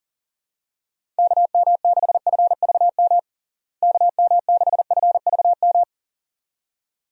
Morse code, heard as a keyed tone, about 37 wpm, 700 Hz. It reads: KM6FVM KM6FVM
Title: USA Call Signs - 6 Characters - Repeated 2x - 40wpm